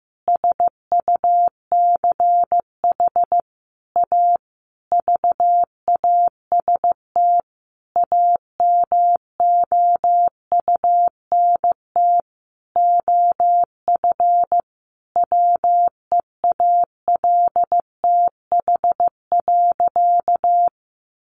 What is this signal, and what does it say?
Morse code, heard as a keyed tone, about 15 wpm, 700 Hz. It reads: SUCH A VAST AMOUNT OF WEALTH.
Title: Sentences from Top 1300 Words 15wpm